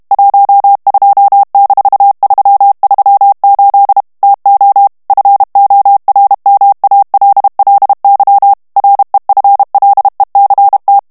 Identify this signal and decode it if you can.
Morse code, heard as a keyed tone, about 32 wpm, 800 Hz. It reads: 12-338 TO FORMALLY REFLECT
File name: morse mp3 / ANGLAIS texte clair 3 vitesse 30 mots minute